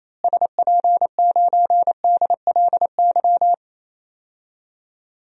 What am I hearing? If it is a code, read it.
Morse code, heard as a keyed tone, about 28 wpm, 700 Hz. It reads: SP9DLY
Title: DXCC-100 Call Signs - 6 Characters - Repeated 3x - 28wpm